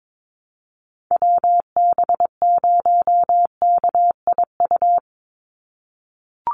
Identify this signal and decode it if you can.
Morse code, heard as a keyed tone, about 22 wpm, 700 Hz. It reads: WB0KIU
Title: NA SOTA Chaser Callsigns 22wpm